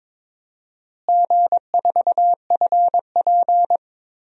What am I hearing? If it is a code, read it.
Morse code, heard as a keyed tone, about 21 wpm, 700 Hz. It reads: G4FP
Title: DXCC-100 Call Signs - 4 Characters - Repeated 3x - 22wpm